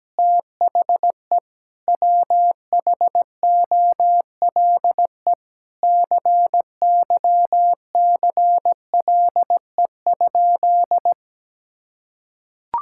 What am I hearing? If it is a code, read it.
Morse code, heard as a keyed tone, about 17 wpm, 700 Hz. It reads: THE WHOLE CYCLE?
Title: Sentences from Top 1000 Words - Review Part 6 - 17wpm